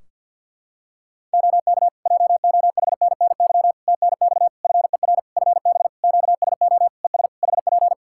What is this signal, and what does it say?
Morse code, heard as a keyed tone, about 48 wpm, 700 Hz. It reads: OK JOHNNY TNX FER FB QSO ES HP
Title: Practice QSO - Exchange 4 - Ending - 50wpm